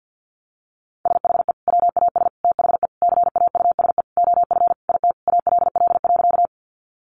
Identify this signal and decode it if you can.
Morse code, heard as a keyed tone, about 50 wpm, 700 Hz. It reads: SHE WAS THE CAUSE OF IT ALL.
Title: Sentences from Top 200 Words 50wpm